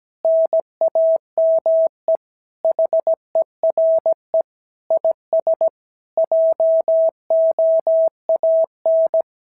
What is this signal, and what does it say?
Morse code, heard as a keyed tone, about 17 wpm, 650 Hz. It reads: NAME HERE IS JOAN
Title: SOTA QSO Practice 17wpm